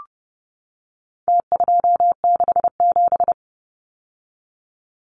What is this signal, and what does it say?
Morse code, heard as a keyed tone, about 30 wpm, 700 Hz. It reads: T267
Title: Numbers - Contest Style - 4-Digit 30wpm